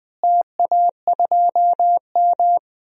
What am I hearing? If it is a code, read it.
Morse code, heard as a keyed tone, about 21 wpm, 700 Hz. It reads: TA2M
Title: DXCC-100 Call Signs - 4 Characters - Repeated 3x - 20wpm